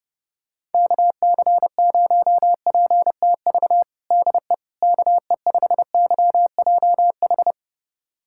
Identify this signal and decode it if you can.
Morse code, heard as a keyed tone, about 30 wpm, 700 Hz. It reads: KC0PTV DE KE5YJH
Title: USA Callsign DE Callsign 30wpm